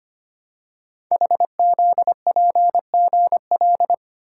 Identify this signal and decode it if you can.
Morse code, heard as a keyed tone, about 25 wpm, 700 Hz. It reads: HZPGL
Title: Code Groups - 5 Letters - 25wpm